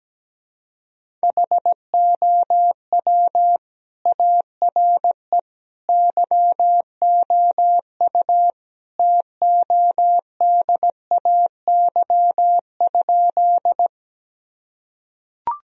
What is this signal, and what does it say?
Morse code, heard as a keyed tone, about 17 wpm, 700 Hz. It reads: HOW ARE YOU TODAY?
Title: Sentences from Top 1000 Words - Review Part 4 - 17wpm